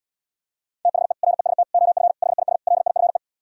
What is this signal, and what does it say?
Morse code, heard as a keyed tone, about 77 wpm, 700 Hz. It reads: THE REST OF HIS LIFE